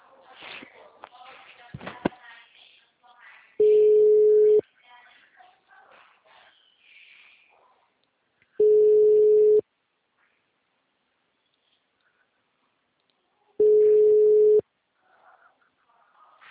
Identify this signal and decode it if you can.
Ringback tone (Europe, 425 Hz), with speech.